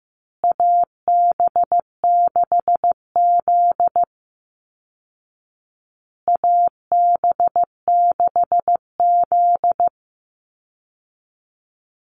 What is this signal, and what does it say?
Morse code, heard as a keyed tone, about 15 wpm, 700 Hz. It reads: AB6Z AB6Z
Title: Call Signs Repeated 2x - 15wpm